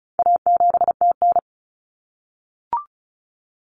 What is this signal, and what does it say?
Morse code, heard as a keyed tone, about 34 wpm, 700 Hz. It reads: A7TN
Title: Common Cut Numbers - 4-Digit 35wpm